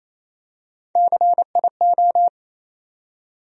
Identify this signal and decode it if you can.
Morse code, heard as a keyed tone, about 28 wpm, 700 Hz. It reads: CIO
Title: USA Call Sign Postfixes - 3 Letters 28wpm